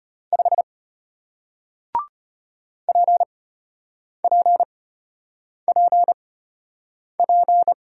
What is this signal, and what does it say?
Morse code, heard as a keyed tone, about 34 wpm, spaced as for 8 wpm, 700 Hz. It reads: 5 PPPP